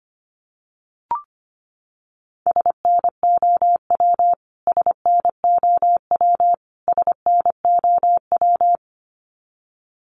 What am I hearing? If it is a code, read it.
Morse code, heard as a keyed tone, about 25 wpm, 700 Hz. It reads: SNOW SNOW SNOW
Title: Top 500 Words Repeated 25wpm